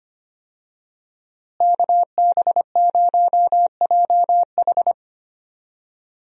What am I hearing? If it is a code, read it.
Morse code, heard as a keyed tone, about 25 wpm, 700 Hz. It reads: KB0JH
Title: USA Call Signs - 5 Characters - Repeated 3x - 25wpm